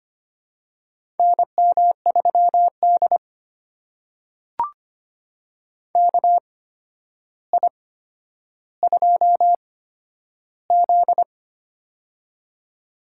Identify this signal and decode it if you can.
Morse code, heard as a keyed tone, about 25 wpm, spaced as for 11 wpm, 700 Hz. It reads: NM3D KI2Z
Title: USA Call Signs - 4 Characters - 8x Character Spacing - 25wpm